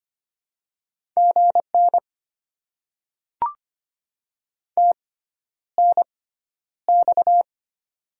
Morse code, heard as a keyed tone, about 25 wpm, spaced as for 11 wpm, 690 Hz. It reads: GN TNX
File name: Top 90 QSO Elements - 6x Character Spacing - 25wpm